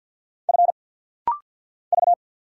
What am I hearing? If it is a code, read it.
Morse code, heard as a keyed tone, about 50 wpm, 700 Hz. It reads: F V
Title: Single Letter-Number Alt - Rapid-Fire 50wpm